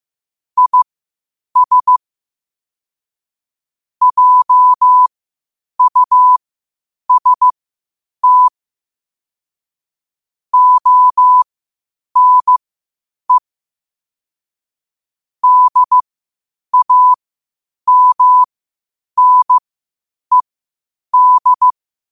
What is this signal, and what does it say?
Morse code, heard as a keyed tone, about 15 wpm, spaced as for 8 wpm, 1000 Hz. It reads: IS JUST ONE DAMNED
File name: qotdfiles / 8wpm